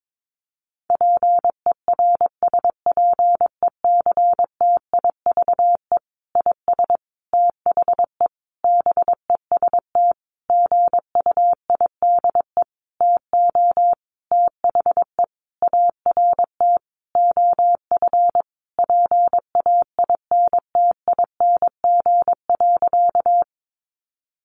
Morse code, heard as a keyed tone, about 22 wpm, 700 Hz. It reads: PERSPECTIVE IS THE BEST GUIDE TO THE ART OF PAINTING.
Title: Sentences from Top 1200 Words 22wpm